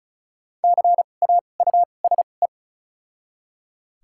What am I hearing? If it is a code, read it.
Morse code, heard as a keyed tone, about 34 wpm, 700 Hz. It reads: C A U S E
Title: 2x Character Spacing - Top 500 Words - 35wpm